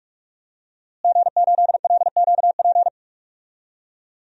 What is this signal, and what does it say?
Morse code, heard as a keyed tone, about 44 wpm, 700 Hz. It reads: G8LQP
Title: DXCC-100 Call Signs - 5 Characters - Repeated 2x - 45wpm